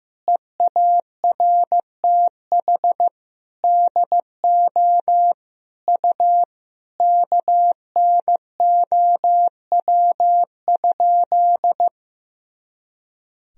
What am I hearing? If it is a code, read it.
Morse code, heard as a keyed tone, about 15 wpm, 700 Hz. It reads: EARTH DO U KNOW?
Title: Sentences from Top 200 Words and Abbreviations 15wpm